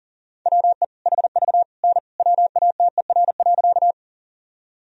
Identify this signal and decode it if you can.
Morse code, heard as a keyed tone, about 40 wpm, 700 Hz. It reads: WE HV N WATER.
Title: Sentences from Top 100 Words and Abbreviations 40wpm